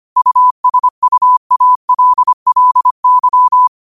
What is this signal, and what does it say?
Morse code, heard as a keyed tone, about 25 wpm, 1000 Hz. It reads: USUALLY